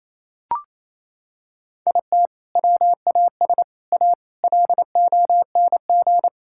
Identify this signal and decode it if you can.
Morse code, heard as a keyed tone, about 28 wpm, 700 Hz. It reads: IT WAS A LONG
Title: Sentences from Top 1600 Words 28wpm